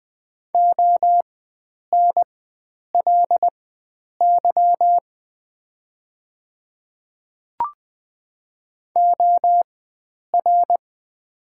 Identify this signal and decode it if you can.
Morse code, heard as a keyed tone, about 20 wpm, spaced as for 9 wpm, 700 Hz. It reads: ONLY OR
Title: 4x Character Spacing - Top 300 Words - 20wpm